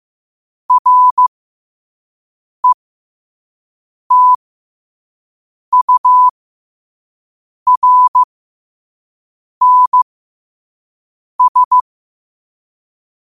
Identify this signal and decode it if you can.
Morse code, heard as a keyed tone, about 15 wpm, spaced as for 5 wpm, 1000 Hz. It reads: RETURNS
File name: qotdfiles / 5wpm